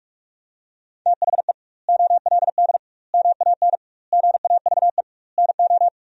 Morse code, heard as a keyed tone, about 41 wpm, 700 Hz. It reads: THE OLD MAN GAVE NO
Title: 2x Word-Spacing - Sentences from Top 500 Words - 45wpm